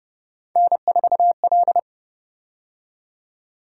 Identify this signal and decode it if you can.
Morse code, heard as a keyed tone, about 29 wpm, 700 Hz. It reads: N4L